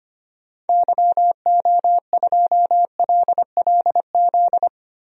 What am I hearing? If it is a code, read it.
Morse code, heard as a keyed tone, about 25 wpm, 700 Hz. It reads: YO2LLZ
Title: DXCC-100 Call Signs - 6 Characters - Repeated 2x - 25wpm